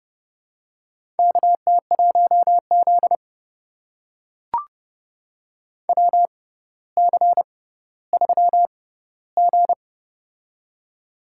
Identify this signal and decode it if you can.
Morse code, heard as a keyed tone, about 30 wpm, spaced as for 15 wpm, 700 Hz. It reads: KT1Z WC3G
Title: USA Call Signs - 4 Characters - 6x Character Spacing - 30wpm